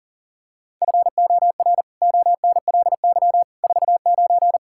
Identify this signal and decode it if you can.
Morse code, heard as a keyed tone, about 40 wpm, 700 Hz. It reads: FOR ONLY 49
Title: Pangram Challenge v2 - 40wpm